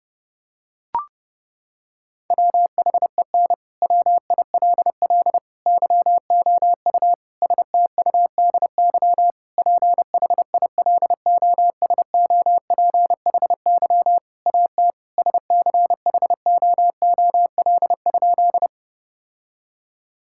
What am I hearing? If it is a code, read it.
Morse code, heard as a keyed tone, about 32 wpm, 700 Hz. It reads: WHEN WILL YOU STUDY PHILOSOPHY AT SCHOOL?
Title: Sentences from Top 1000 Words - Review Part 6 - 30wpm